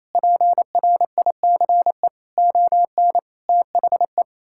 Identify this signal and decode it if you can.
Morse code, heard as a keyed tone, about 28 wpm, 700 Hz. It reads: PRICE ON THE
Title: Sentences from Top 1000 Words - Review Part 6 - 28wpm